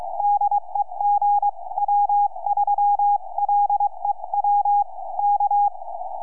Morse code, heard as a keyed tone, about 23 wpm, spaced as for 20 wpm, 820 Hz, with radio noise. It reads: DE GW3LEW K
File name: gw3lew 4m